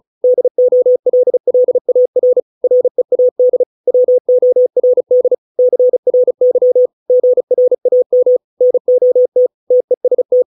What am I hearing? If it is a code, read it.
Morse code, heard as a keyed tone, about 34 wpm, 500 Hz. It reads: DOLLAR READ WORD CRY GRAM NOT TEST